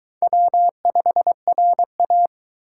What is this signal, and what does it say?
Morse code, heard as a keyed tone, about 20 wpm, 700 Hz. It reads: W5RA
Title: Speed Racing - USA 1x2 Call Signs 17wpm